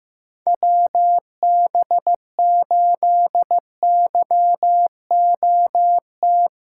Morse code, heard as a keyed tone, about 15 wpm, 700 Hz. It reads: WB8YOT
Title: USA Callsign DE Callsign 15wpm